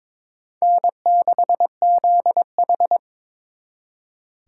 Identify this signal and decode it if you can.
Morse code, heard as a keyed tone, about 26 wpm, 700 Hz. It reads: N6ZH
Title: Speed Racing - USA 1x2 Call Signs 22wpm